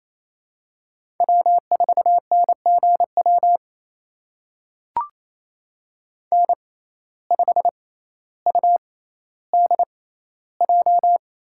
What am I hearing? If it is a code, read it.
Morse code, heard as a keyed tone, about 28 wpm, spaced as for 14 wpm, 700 Hz. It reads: W4NGW N5UDJ
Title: USA Call Signs - 5 Characters - 6x Character Spacing - 28wpm